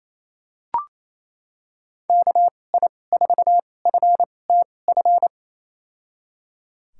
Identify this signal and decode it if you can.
Morse code, heard as a keyed tone, about 28 wpm, spaced as for 20 wpm, 700 Hz. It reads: KI4FTF
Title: USA Call Signs - 2x Character Spacing - 28wpm